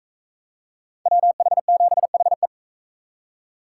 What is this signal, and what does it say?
Morse code, heard as a keyed tone, about 40 wpm, 700 Hz. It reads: WH7HE